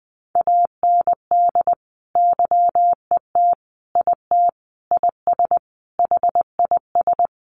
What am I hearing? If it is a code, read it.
Morse code, heard as a keyed tone, about 20 wpm, 700 Hz. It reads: AND YET IT IS HIS
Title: Sentences from Top 800 Words 20wpm